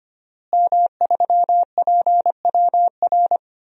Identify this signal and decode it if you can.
Morse code, heard as a keyed tone, about 25 wpm, 700 Hz. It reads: M3PWR
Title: DXCC-100 Call Signs - 5 Characters - Repeated 2x - 25wpm